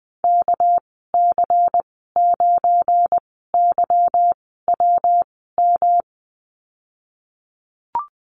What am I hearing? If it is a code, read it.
Morse code, heard as a keyed tone, about 20 wpm, spaced as for 15 wpm, 700 Hz. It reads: KC9YWM
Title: 231 Lesson 37g - Letter x - Call Signs 20wpm